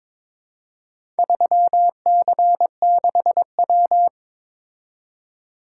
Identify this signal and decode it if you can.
Morse code, heard as a keyed tone, about 22 wpm, 700 Hz. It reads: 3C6W
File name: practice / 22wpm / Four Letter-Number Combinations - 22wpm